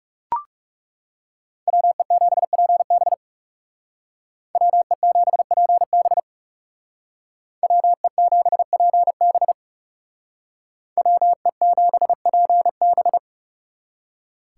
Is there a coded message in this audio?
Yes: Morse code from 1.7 to 13.2 s, heard as a keyed tone, about 38 wpm, 700 Hz. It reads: WE7PB WE7PB WE7PB WE7PB